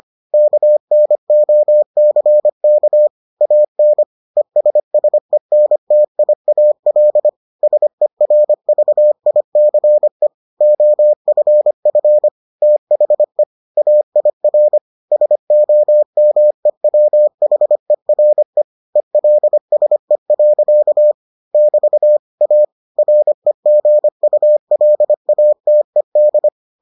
Morse code, heard as a keyed tone, about 25 wpm, 600 Hz. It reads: KNOCK AN ESSENTIAL SERVICE OFF THE AIR SOMEWHERE ELSE. <BT> A REGULATED